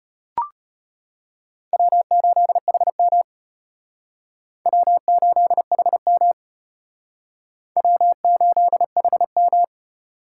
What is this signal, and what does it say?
Morse code, heard as a keyed tone, about 34 wpm, 700 Hz. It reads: W8HM W8HM W8HM